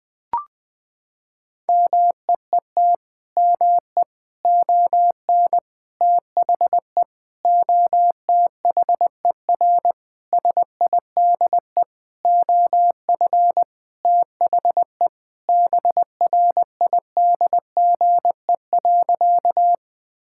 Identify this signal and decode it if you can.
Morse code, heard as a keyed tone, about 20 wpm, 700 Hz. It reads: MEET ME ON THE OTHER SIDE OF THE BRIDGE.
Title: Sentences with 6-Word Phrases - 20wpm